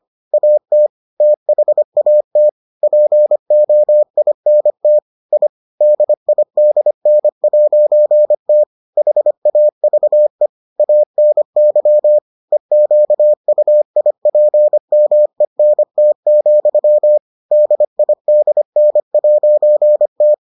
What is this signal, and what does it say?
Morse code, heard as a keyed tone, about 25 wpm, 600 Hz. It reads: AT THAT POINT I DIDN'T HAVE ANY EQUIPMENT, DIDN'T